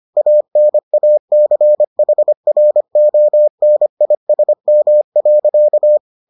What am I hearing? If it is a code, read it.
Morse code, heard as a keyed tone, about 25 wpm, 600 Hz. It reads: ANACHRONISM.